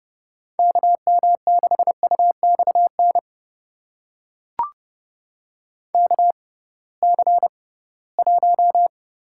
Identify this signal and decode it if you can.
Morse code, heard as a keyed tone, about 30 wpm, spaced as for 19 wpm, 700 Hz. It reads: KM6UXN KC1
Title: USA Call Signs - 6x Character Spacing - 30wpm